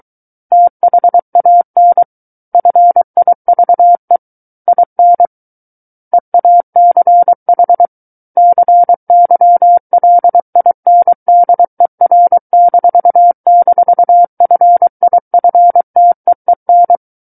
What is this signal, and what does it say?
Morse code, heard as a keyed tone, about 23 wpm, 700 Hz. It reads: THAN FIVE IN EACH CYLINDER--FIFTEEN
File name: Book One / 14-Chapter 14 - 23 WPM